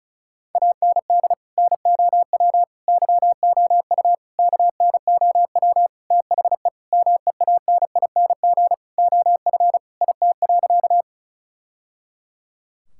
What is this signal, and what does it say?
Morse code, heard as a keyed tone, about 36 wpm, 700 Hz. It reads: AND NOW YOU KNOW THE MEANING OF IT.
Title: Sentences from Top 1000 Words - Review Part 6 - 35wpm